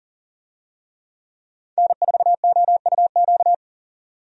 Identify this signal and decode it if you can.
Morse code, heard as a keyed tone, about 40 wpm, 700 Hz. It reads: N4OUQ
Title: USA Call Signs - 5 Characters - 40wpm